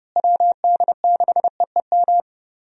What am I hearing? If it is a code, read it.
Morse code, heard as a keyed tone, about 30 wpm, 700 Hz. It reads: WD6EEM
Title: USA Call Signs - 6 Characters - Repeated 2x - 30wpm